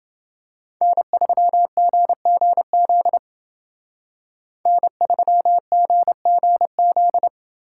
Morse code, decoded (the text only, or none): N3GGZ N3GGZ